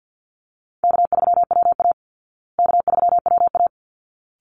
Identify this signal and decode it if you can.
Morse code, heard as a keyed tone, about 50 wpm, 700 Hz. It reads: K3WA K3WA
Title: Call Signs Repeated 3x - 50wpm